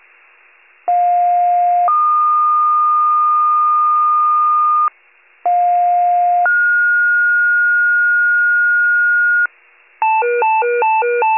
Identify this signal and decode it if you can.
Two-tone page, heard as a one-second tone, then a three-second tone, about 710 then 1220 Hz, about 710 then 1430 Hz, with noise. It is heard from 0.9 to 9.5 s.
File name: 46 FIRE